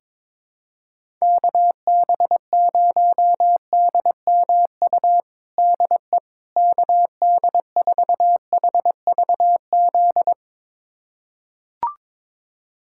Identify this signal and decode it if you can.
Morse code, heard as a keyed tone, about 22 wpm, 700 Hz. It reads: KB0DMU DE KD4HVZ
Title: USA Callsign DE Callsign 22wpm